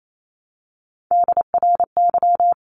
Morse code, heard as a keyed tone, about 28 wpm, 700 Hz. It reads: DRY
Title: Top 500 Words Repeated 28wpm